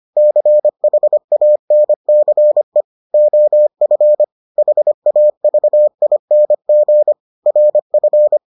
Morse code, heard as a keyed tone, about 25 wpm, 600 Hz. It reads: CHANCE OF HAVING RF